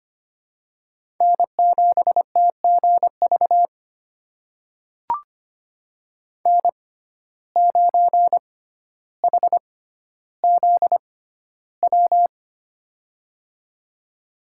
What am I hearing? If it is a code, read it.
Morse code, heard as a keyed tone, about 25 wpm, spaced as for 13 wpm, 700 Hz. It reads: N7TGV N9HZW
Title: USA Call Signs - 5 Characters - 6x Character Spacing - 25wpm